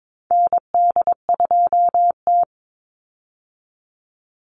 Morse code, heard as a keyed tone, about 22 wpm, 690 Hz. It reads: ND2T
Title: Call Signs Repeated 2x - 22wpm